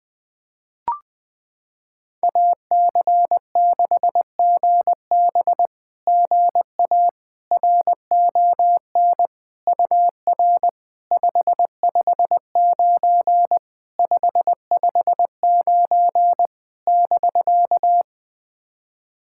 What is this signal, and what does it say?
Morse code, heard as a keyed tone, about 20 wpm, 700 Hz. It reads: AC6GB GA RON UR 559 559 <BK>
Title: SOTA QSO Practice 20wpm